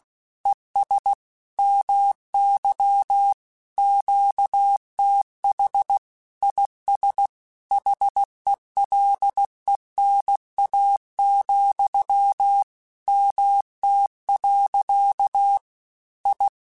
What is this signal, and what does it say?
Morse code, heard as a keyed tone, about 16 wpm, 800 Hz. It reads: ES MY QTH IS HELENA, MT. I
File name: QSOs / QSOs-mp3s / WD0BGP 1@15